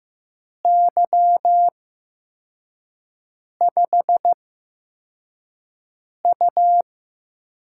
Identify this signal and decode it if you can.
Morse code, heard as a keyed tone, about 15 wpm, spaced as for 4 wpm, 700 Hz. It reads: Y5U
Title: USA Call Signs - 4 Characters - 8x Character Spacing - 15wpm